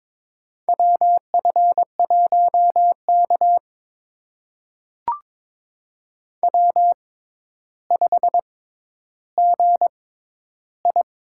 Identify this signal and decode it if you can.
Morse code, heard as a keyed tone, about 22 wpm, spaced as for 11 wpm, 700 Hz. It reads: WF1K W5GI
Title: USA Call Signs - 4 Characters - 6x Character Spacing - 22wpm